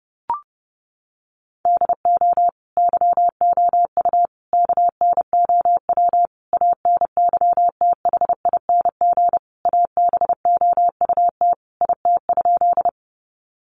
Morse code, heard as a keyed tone, about 30 wpm, 700 Hz. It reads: DO YOU KNOW ANYTHING ABOUT IT?
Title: Sentences from Top 1000 Words 30wpm